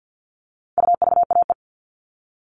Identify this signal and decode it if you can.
Morse code, heard as a keyed tone, about 50 wpm, 700 Hz. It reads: UVAE
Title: All Cut Numbers - 4-Digit 50wpm